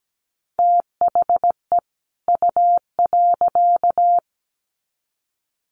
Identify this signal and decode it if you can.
Morse code, heard as a keyed tone, about 17 wpm, 700 Hz. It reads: THE U.
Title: Sets of 2 Words 17wpm